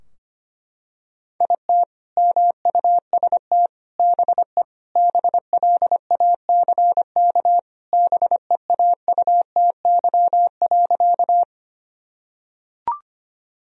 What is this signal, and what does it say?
Morse code, heard as a keyed tone, about 25 wpm, 700 Hz. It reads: IT MUST BE BLACK BEAUTY.
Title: Sentences from Top 1000 Words - Review Part 2 - 25wpm